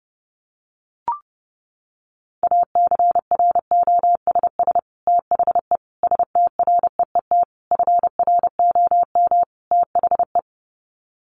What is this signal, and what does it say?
Morse code, heard as a keyed tone, about 30 wpm, 700 Hz. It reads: ACROSS THE STREET FROM THE
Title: Sets of 5 Words Encore 30wpm